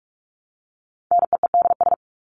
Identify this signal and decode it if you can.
Morse code, heard as a keyed tone, about 44 wpm, 700 Hz. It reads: NEEDS
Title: ES:S Suffix - Top 100 Words with Base Words 45wpm